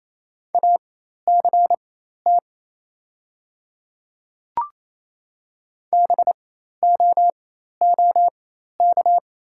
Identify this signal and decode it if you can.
Morse code, heard as a keyed tone, about 28 wpm, spaced as for 13 wpm, 700 Hz. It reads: ACT BOOK
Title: 4x Character Spacing - Top 300 Words - 28wpm